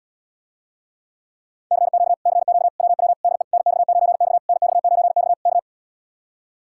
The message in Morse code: CQ CQ WW DE NX9X NX9X K

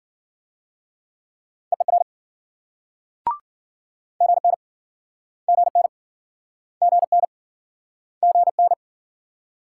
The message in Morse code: EER GN GN GN GN

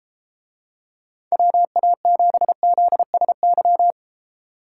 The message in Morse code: WA7ZSY

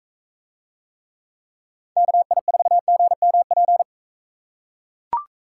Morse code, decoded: KI4GMP

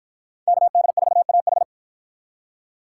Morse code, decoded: KD4AH

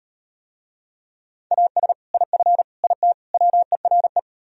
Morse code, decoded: AS IF IT WERE

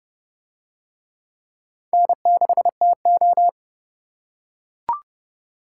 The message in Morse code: N6TO

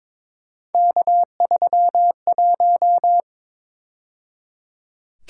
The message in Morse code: K31

